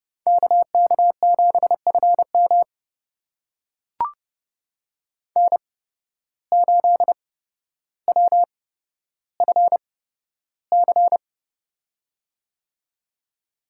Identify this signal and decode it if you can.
Morse code, heard as a keyed tone, about 29 wpm, spaced as for 13 wpm, 700 Hz. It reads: KK7FM N8WFC
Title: USA Call Signs - 5 Characters - 8x Character Spacing - 30wpm